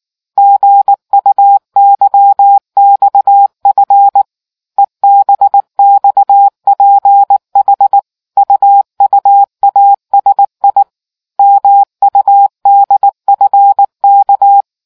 Morse code, heard as a keyed tone, about 19 wpm, 800 Hz. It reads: GUYXF EBXPH UUASI MUDFK